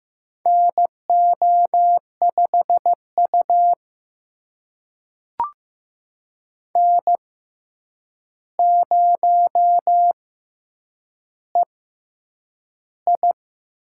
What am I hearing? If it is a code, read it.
Morse code, heard as a keyed tone, about 15 wpm, spaced as for 8 wpm, 700 Hz. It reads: NO5U N0EI
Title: USA Call Signs - 4 Characters - 6x Character Spacing - 15wpm